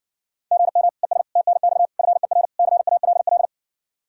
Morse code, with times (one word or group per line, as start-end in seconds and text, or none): 0.5-0.9 GM
1.0-1.2 ES
1.4-1.9 TNX
2.0-2.5 FER
2.6-3.5 CALL